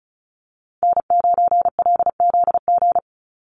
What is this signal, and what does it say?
Morse code, heard as a keyed tone, about 34 wpm, 700 Hz. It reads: N9LZG
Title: USA Call Signs 35wpm